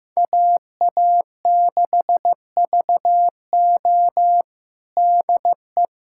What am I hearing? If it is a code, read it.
Morse code, heard as a keyed tone, about 15 wpm, 700 Hz. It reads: AA6VO DE
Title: USA Callsign DE Callsign - 5 Characters - 15wpm